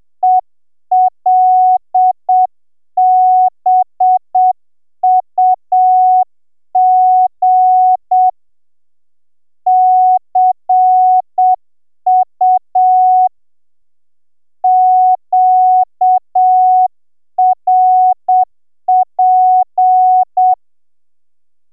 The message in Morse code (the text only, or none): ELBUG CU QRP